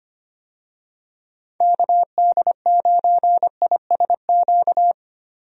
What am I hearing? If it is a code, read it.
Morse code, heard as a keyed tone, about 25 wpm, 700 Hz. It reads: KD9ISQ
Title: USA Call Signs - 6 Characters - Repeated 3x - 25wpm